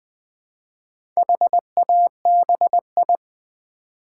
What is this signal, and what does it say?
Morse code, heard as a keyed tone, about 20 wpm, 700 Hz. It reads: HABI
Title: Build A Word - 1 Letter At A Time - 5-Letter Words - 20wpm